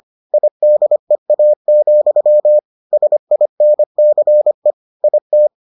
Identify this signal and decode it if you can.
Morse code, heard as a keyed tone, about 25 wpm, 600 Hz. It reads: IDEA, SINCE IT